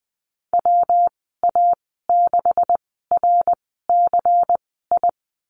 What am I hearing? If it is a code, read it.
Morse code, heard as a keyed tone, about 20 wpm, spaced as for 15 wpm, 700 Hz. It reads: WA6RCI
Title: 179 Lesson 29g - Slash - Call Signs 20wpm